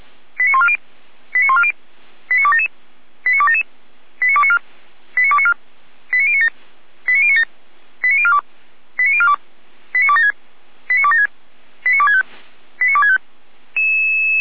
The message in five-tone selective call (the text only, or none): 89140 89140 89277 89277 89294 89294 89097 89097 89042 89042 89276 89276 89265 89265